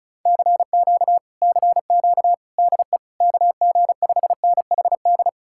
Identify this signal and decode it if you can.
Morse code, heard as a keyed tone, about 46 wpm, 700 Hz. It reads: CQ CQ DE KG5NHD